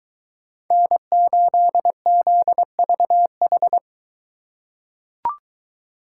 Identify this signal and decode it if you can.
Morse code, heard as a keyed tone, about 23 wpm, 700 Hz. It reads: N8ZVH